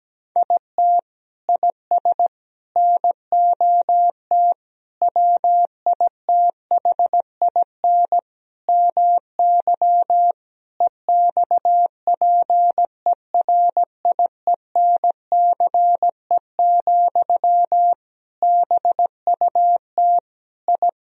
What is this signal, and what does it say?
Morse code, heard as a keyed tone, about 17 wpm, 700 Hz. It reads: IT IS NOT WITHIN MY EXPERIENCE, BUT I
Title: Sentences from Top 1000 Words - Review Part 4 - 17wpm